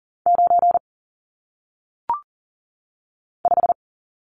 Morse code, heard as a keyed tone, about 41 wpm, 700 Hz. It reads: 9 5
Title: Numbers - 1-Digit 40wpm